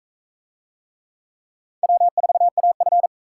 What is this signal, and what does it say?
Morse code, heard as a keyed tone, about 44 wpm, 700 Hz. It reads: W4AF